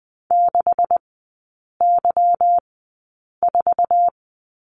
Morse code, heard as a keyed tone, about 20 wpm, spaced as for 8 wpm, 690 Hz. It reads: 6Y4
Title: 159 Lesson 26e - Letter g - Three Characters 20wpm